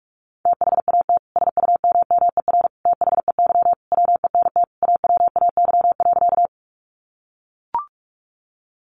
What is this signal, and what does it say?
Morse code, heard as a keyed tone, about 44 wpm, 700 Hz. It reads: THAT SUMMER THEY WENT AWAY.
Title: Sentences from Top 1100 Words 45wpm